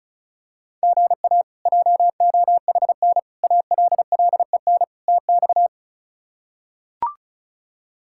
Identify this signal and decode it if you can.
Morse code, heard as a keyed tone, about 34 wpm, 700 Hz. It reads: GA JOHN ALLEN TX
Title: K1USN SST QSO Practice - 35wpm